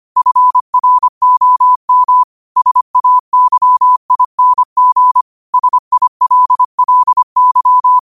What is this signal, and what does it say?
Morse code, heard as a keyed tone, about 25 wpm, 1000 Hz. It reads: FROM SAYING SILLY
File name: qotdfiles / 25wpm